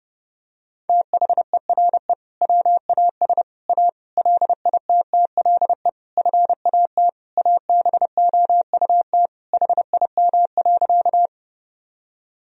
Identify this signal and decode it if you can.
Morse code, heard as a keyed tone, about 30 wpm, 700 Hz. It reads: THERE WAS A LITTLE FAT ABOUT HIM.
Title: Sentences from Top 1000 Words - Review Part 4 - 30wpm